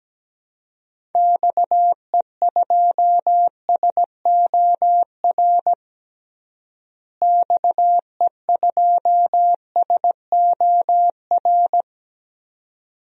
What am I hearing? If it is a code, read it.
Morse code, heard as a keyed tone, about 17 wpm, 700 Hz. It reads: XE2SOR XE2SOR
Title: DXCC-100 Call Signs - 6 Characters - Repeated 2x - 17wpm